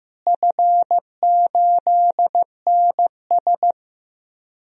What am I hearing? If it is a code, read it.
Morse code, heard as a keyed tone, about 15 wpm, 700 Hz. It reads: F8NS
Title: DXCC-100 Call Signs - 4 Characters - Repeated 2x - 15wpm